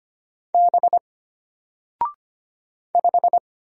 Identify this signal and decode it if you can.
Morse code, heard as a keyed tone, about 25 wpm, 700 Hz. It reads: B 5
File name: Single-Letter-Number-25wpm